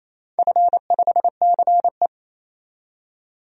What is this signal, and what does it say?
Morse code, heard as a keyed tone, about 28 wpm, 700 Hz. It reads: F5CE
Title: DXCC-100 Call Signs - 4 Characters - 28wpm